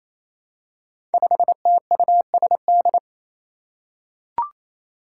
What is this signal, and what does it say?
Morse code, heard as a keyed tone, about 28 wpm, 700 Hz. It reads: 5TUSD
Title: Five Challenging Letter-Number Combinations 28wpm